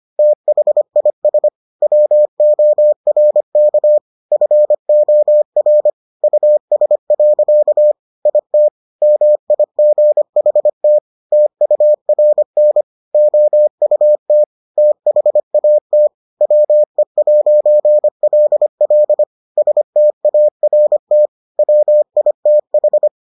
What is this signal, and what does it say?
Morse code, heard as a keyed tone, about 25 wpm, 600 Hz. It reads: THIS WORK FOR US. IT MIGHT TURN OUT THAT WE'LL START WITH